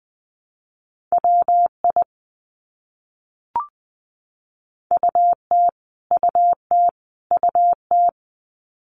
Morse code, heard as a keyed tone, about 20 wpm, 700 Hz. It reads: WI UT UT UT
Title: US States Repeated 3x 20wpm